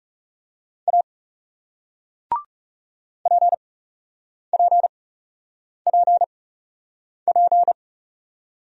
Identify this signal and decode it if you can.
Morse code, heard as a keyed tone, about 37 wpm, spaced as for 8 wpm, 700 Hz. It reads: A PPPP
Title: Speed-Racing Single Letter-Number 30wpm